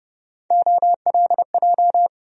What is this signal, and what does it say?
Morse code, heard as a keyed tone, about 30 wpm, 700 Hz. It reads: OLJ
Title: USA Call Sign Postfixes - 3 Letters 30wpm